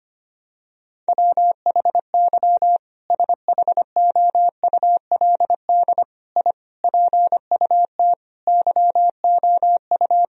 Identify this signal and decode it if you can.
Morse code, heard as a keyed tone, about 25 wpm, 700 Hz. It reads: WHY SHOULD I PUT YOU
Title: Sentences from Top 1000 Words - Review Part 4 - 25wpm